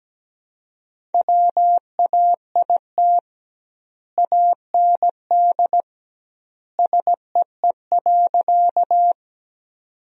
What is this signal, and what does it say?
Morse code, heard as a keyed tone, about 17 wpm, 700 Hz. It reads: WAIT AND SEE.
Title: 2x Word-Spacing - Sentences from Top 500 Words - 17wpm